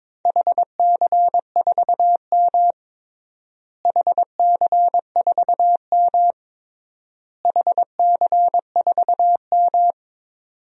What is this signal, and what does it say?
Morse code, heard as a keyed tone, about 22 wpm, 700 Hz. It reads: HC4M HC4M HC4M
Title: DXCC-100 Call Signs - 4 Characters - Repeated 3x - 22wpm